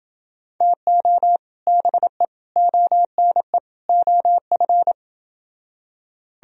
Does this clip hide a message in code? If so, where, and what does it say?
Yes: Morse code from 0.6 to 4.9 s, heard as a keyed tone, about 28 wpm, 700 Hz. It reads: TO BE ONE OF